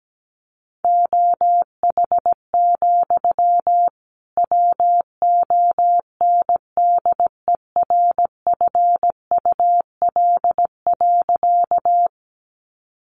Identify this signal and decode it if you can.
Morse code, heard as a keyed tone, about 17 wpm, 700 Hz. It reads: OH, WONDERFUL.
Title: Sentences from Top 1100 Words 17wpm